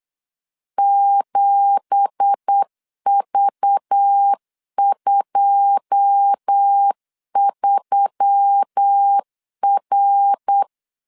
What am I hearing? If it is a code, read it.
Morse code, heard as a keyed tone, about 8 wpm, 800 Hz. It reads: 7V23R
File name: 8wpm